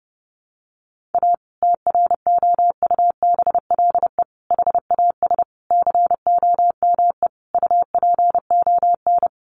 Morse code, heard as a keyed tone, about 30 wpm, 700 Hz. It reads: A TROUBLE HAS COME UPON
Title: Sentences from Top 1200 Words 30wpm